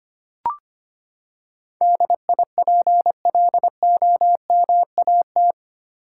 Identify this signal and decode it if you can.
Morse code, heard as a keyed tone, about 25 wpm, 700 Hz. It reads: DIPLOMAT
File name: Eight Letter Words - 25wpm